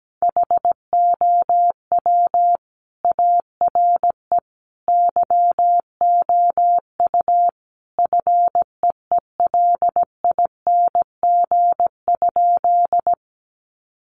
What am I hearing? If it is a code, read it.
Morse code, heard as a keyed tone, about 17 wpm, 700 Hz. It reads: HOW ARE YOU FEELING?
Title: Sentences from Top 1200 Words 17wpm